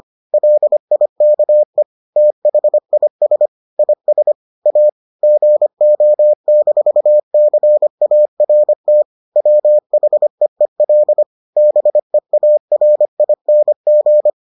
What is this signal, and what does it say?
Morse code, heard as a keyed tone, about 25 wpm, 600 Hz. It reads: LIKE THIS IS A GO-CART WHEEL BEARING